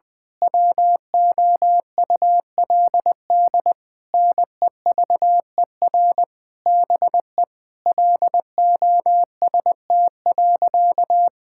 Morse code, heard as a keyed tone, about 20 wpm, 700 Hz. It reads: WOULD NEVER BE LOST.
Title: Sentences from Top 1600 Words 20wpm